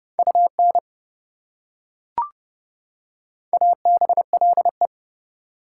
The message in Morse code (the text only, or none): UN ABLE